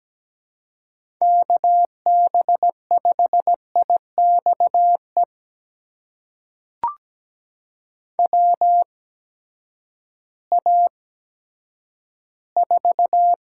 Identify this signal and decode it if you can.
Morse code, heard as a keyed tone, about 17 wpm, spaced as for 10 wpm, 700 Hz. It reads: KB5IXE WA4
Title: USA Call Signs - 6 Characters - 8x Character Spacing - 17wpm